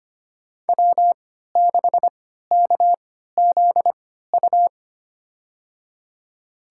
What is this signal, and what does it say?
Morse code, heard as a keyed tone, about 25 wpm, spaced as for 14 wpm, 700 Hz. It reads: W6KZU
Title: USA Call Signs - 5 Characters - 3x Character Spacing - 25wpm